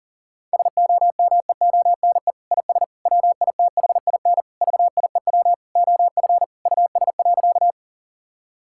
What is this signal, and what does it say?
Morse code, heard as a keyed tone, about 40 wpm, 700 Hz. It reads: SOMEONE IS WITHIN VIEW OF US.
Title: Sentences from Top 1000 Words - Review Part 6 - 40wpm